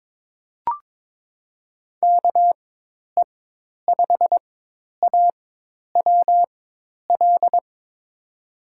Morse code, heard as a keyed tone, about 21 wpm, spaced as for 10 wpm, 700 Hz. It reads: KE5AWL
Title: USA Call Signs - 4x Character Spacing - 22wpm